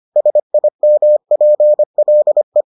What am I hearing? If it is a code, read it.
Morse code, heard as a keyed tone, about 25 wpm, 600 Hz. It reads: SIMPLE